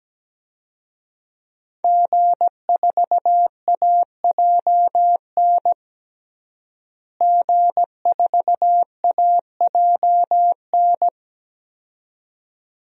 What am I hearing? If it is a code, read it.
Morse code, heard as a keyed tone, about 17 wpm, 700 Hz. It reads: G4AJN G4AJN
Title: DXCC-100 Call Signs - 5 Characters - Repeated 2x - 17wpm